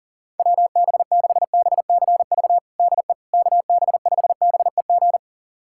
Morse code, heard as a keyed tone, about 40 wpm, 700 Hz. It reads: WB6BCV DE KB5BEG
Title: USA Callsign DE Callsign - 6 Characters - 40wpm